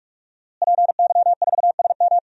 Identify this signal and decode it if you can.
Morse code, heard as a keyed tone, about 45 wpm, 700 Hz. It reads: PY4SM
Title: DXCC-100 Call Signs - 5 Characters - Repeated 3x - 45wpm